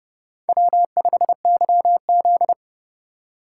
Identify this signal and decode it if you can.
Morse code, heard as a keyed tone, about 28 wpm, 700 Hz. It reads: W5YZ